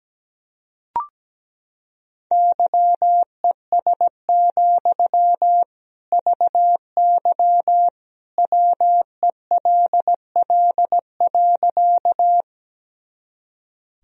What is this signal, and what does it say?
Morse code, heard as a keyed tone, about 17 wpm, 700 Hz. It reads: YES, VY WELL.
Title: Sentences from Top 500 Words and Abbreviations 17wpm